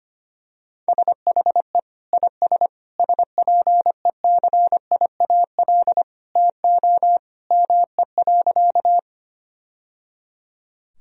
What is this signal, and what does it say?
Morse code, heard as a keyed tone, about 25 wpm, 700 Hz. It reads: SHE IS SPECIAL TO ME.
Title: Sentences from Top 1000 Words - Review Part 3 - 25wpm